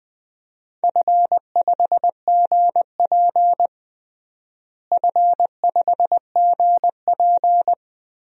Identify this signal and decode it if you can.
Morse code, heard as a keyed tone, about 20 wpm, 700 Hz. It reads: F5GP F5GP
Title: DXCC-100 Call Signs - 4 Characters - Repeated 2x - 20wpm